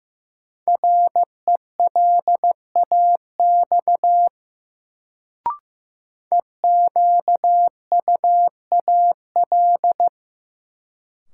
Morse code, heard as a keyed tone, about 15 wpm, 700 Hz. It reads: RELAX EQUAL
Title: Sentences from Top 1400 Words - New Word Drill - 15wpm